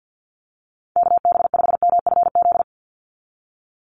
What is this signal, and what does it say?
Morse code, heard as a keyed tone, about 50 wpm, 700 Hz. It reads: KB5MFZ
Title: USA Call Signs 50wpm